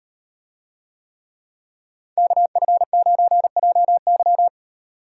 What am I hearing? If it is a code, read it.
Morse code, heard as a keyed tone, about 35 wpm, 700 Hz. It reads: KF9JY